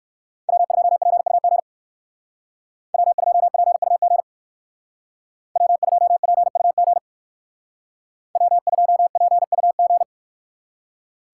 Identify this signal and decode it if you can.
Morse code, heard as a keyed tone, about 58 wpm, 700 Hz. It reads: W2PUG W2PUG W2PUG W2PUG